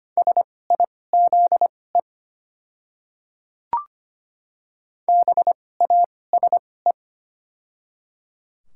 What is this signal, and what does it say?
Morse code, heard as a keyed tone, about 25 wpm, spaced as for 18 wpm, 700 Hz. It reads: SIZE BASE
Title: 2x Character Spacing - Top 500 Words - 25wpm